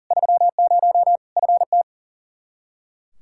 Morse code, heard as a keyed tone, about 40 wpm, 700 Hz. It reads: 30 FT
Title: Practice QSO - Exchange 2 - Info - 40wpm